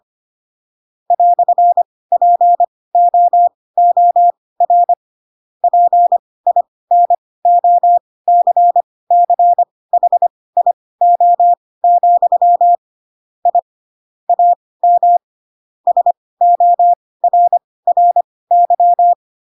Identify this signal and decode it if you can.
Morse code, heard as a keyed tone, about 25 wpm, spaced as for 18 wpm, 700 Hz. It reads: "POOR PINOCCHIO, I AM SORRY
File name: Pinocchio / Pinocchio ch4 12 25 ik1hge